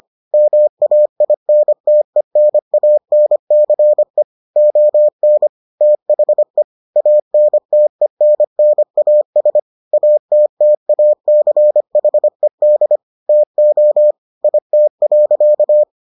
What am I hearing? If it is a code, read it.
Morse code, heard as a keyed tone, about 25 wpm, 600 Hz. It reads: MAINTENANCE ON THE ANTENNAS ATTACHED TO IT.